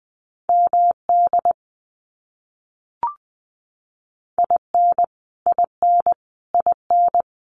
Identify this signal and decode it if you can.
Morse code, heard as a keyed tone, about 20 wpm, 700 Hz. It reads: MD IN IN IN
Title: US States Repeated 3x 20wpm